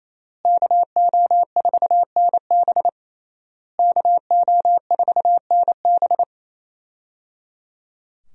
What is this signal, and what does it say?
Morse code, heard as a keyed tone, about 27 wpm, 700 Hz. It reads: KO4NB KO4NB
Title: USA Call Signs - 5 Characters - Repeated 3x - 28wpm